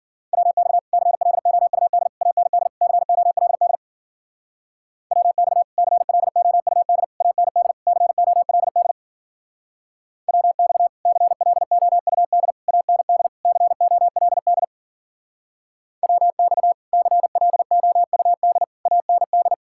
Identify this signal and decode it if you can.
Morse code, heard as a keyed tone, about 51 wpm, 700 Hz. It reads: WX CLOUD AND COLD WX CLOUD AND COLD WX CLOUD AND COLD WX CLOUD AND